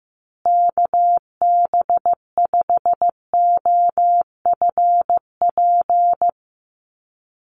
Morse code, heard as a keyed tone, about 15 wpm, 700 Hz. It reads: KB5OFP
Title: USA Call Signs Repeated 2x 15wpm